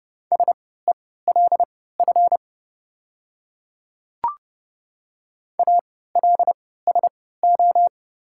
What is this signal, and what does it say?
Morse code, heard as a keyed tone, about 30 wpm, spaced as for 17 wpm, 700 Hz. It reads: SELF ALSO
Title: 3x Character Spacing - Top 200 Words - 30wpm